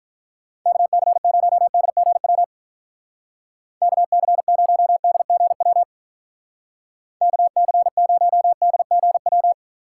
Morse code, heard as a keyed tone, about 44 wpm, 700 Hz. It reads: KC0DGW KC0DGW KC0DGW